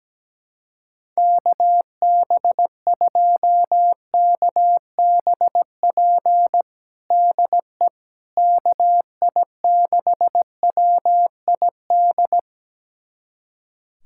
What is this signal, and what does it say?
Morse code, heard as a keyed tone, about 17 wpm, 700 Hz. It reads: KB2KBP DE KI6WID
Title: USA Callsign DE Callsign - 6 Characters - 17wpm